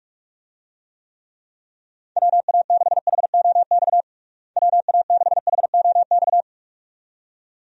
Morse code, heard as a keyed tone, about 46 wpm, 700 Hz. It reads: WA6HOX WA6HOX
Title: USA Call Signs - 6 Characters - Repeated 2x - 45wpm